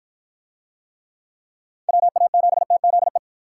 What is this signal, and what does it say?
Morse code, heard as a keyed tone, about 49 wpm, 700 Hz. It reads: WA7TZE